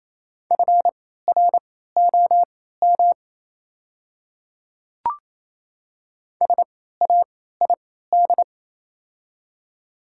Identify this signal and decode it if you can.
Morse code, heard as a keyed tone, about 28 wpm, spaced as for 16 wpm, 700 Hz. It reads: FROM SAID
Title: 3x Character Spacing - Top 300 Words - 28wpm